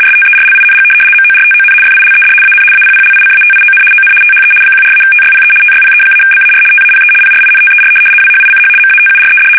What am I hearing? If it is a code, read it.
RTTY, about 75 baud. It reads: STERLING MORTON FILED AN APPLICATION ON THE 14-TYPE TYPE-BAR TAPE PRINTER WHICH MATURED
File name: fsk850-75B / teletype-corp